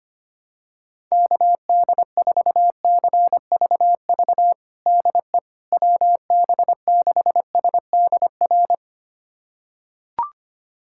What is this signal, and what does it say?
Morse code, heard as a keyed tone, about 25 wpm, 700 Hz. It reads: KD4CVV DE WB6SDR
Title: USA Callsign DE Callsign - 6 Characters - 25wpm